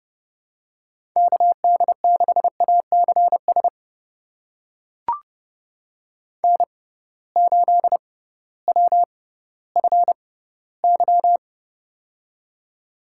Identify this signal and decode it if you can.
Morse code, heard as a keyed tone, about 31 wpm, spaced as for 16 wpm, 700 Hz. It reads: KD6ACS N8WFY
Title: USA Call Signs - 6x Character Spacing - 30wpm